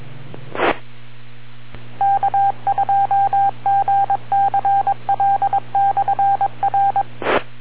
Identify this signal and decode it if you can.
Morse code, heard as a keyed tone, about 22 wpm, 790 Hz. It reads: E K2GCL/R